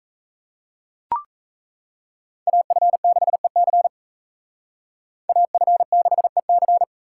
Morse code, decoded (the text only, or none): AF6EC AF6EC